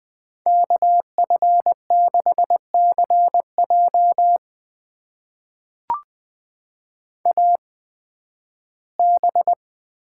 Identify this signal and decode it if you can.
Morse code, heard as a keyed tone, about 20 wpm, spaced as for 13 wpm, 700 Hz. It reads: KF6CJ AB
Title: USA Call Signs - 5 Characters - 8x Character Spacing - 20wpm